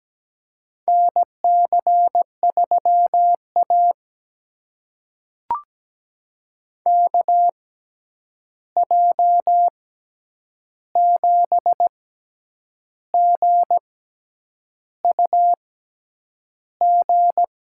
Morse code, heard as a keyed tone, about 17 wpm, spaced as for 8 wpm, 700 Hz. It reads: NC3A KJ7GUG
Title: USA Call Signs - 6x Character Spacing - 17wpm